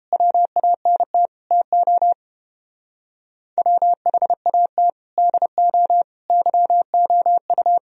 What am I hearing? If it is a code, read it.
Morse code, heard as a keyed tone, about 29 wpm, 700 Hz. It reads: WANT TO WHAT DO YOU